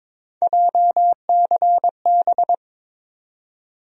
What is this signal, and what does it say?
Morse code, heard as a keyed tone, about 21 wpm, 700 Hz. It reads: JCB
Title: USA Call Sign Postfixes - 3 Letters 22wpm